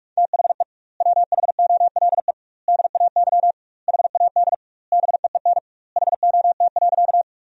Morse code, heard as a keyed tone, about 46 wpm, 700 Hz. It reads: THE WHOLE DAY HAD BEEN HOT.
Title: 2x Word-Spacing - Sentences from Top 500 Words - 45wpm